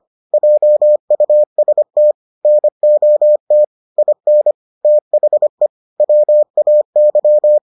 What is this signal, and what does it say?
Morse code, heard as a keyed tone, about 25 wpm, 600 Hz. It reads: JUST NOT IN THE WAY